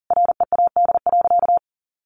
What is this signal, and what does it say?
Morse code, heard as a keyed tone, about 40 wpm, 700 Hz. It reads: READ.
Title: Sentences from Top 1000 Words 40wpm